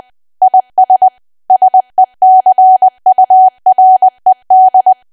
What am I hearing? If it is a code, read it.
Morse code, heard as a keyed tone, about 20 wpm, 750 Hz. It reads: IS SECURED
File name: cwa / W1AW-Level3 / 20WPM